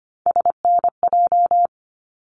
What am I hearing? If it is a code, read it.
Morse code, heard as a keyed tone, about 25 wpm, 700 Hz. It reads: SNJ
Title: ARRL Sections 25wpm